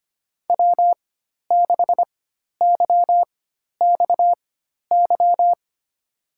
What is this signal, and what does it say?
Morse code, heard as a keyed tone, about 25 wpm, spaced as for 12 wpm, 700 Hz. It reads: W6YXY
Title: USA Call Signs - 5 Characters - 4x Character Spacing - 25wpm